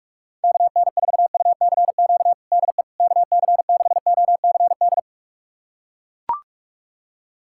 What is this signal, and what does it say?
Morse code, heard as a keyed tone, about 41 wpm, 690 Hz. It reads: KN4UCQ DE KC6OCD
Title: USA Callsign DE Callsign - 6 Characters - 45wpm